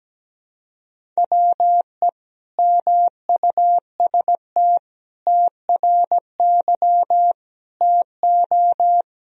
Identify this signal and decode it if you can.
Morse code, heard as a keyed tone, about 17 wpm, 700 Hz. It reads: WE MUST TRY TO
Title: Sentences from Top 400 Words and Abbreviations 17wpm